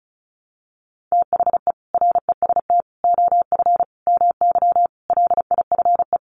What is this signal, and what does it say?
Morse code, heard as a keyed tone, about 34 wpm, 700 Hz. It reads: THE REST OF MY LIFE
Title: Sets of 5 Words 35wpm